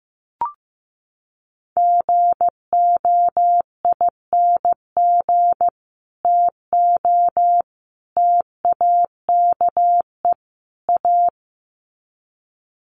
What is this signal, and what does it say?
Morse code, heard as a keyed tone, about 15 wpm, 700 Hz. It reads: GOING TO TAKE A
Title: Sets of 4 Words 15wpm